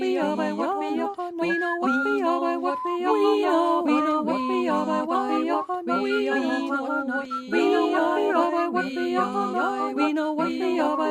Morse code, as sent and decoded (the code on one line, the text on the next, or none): none